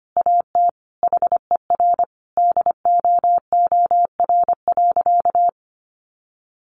Morse code, heard as a keyed tone, about 25 wpm, 700 Hz. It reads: AT HER DOOR.
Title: Sentences from Top 1400 Words 25wpm